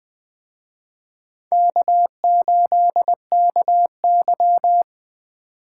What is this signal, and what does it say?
Morse code, heard as a keyed tone, about 24 wpm, 700 Hz. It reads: K8KY
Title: Speed Racing - USA 1x2 Call Signs 20wpm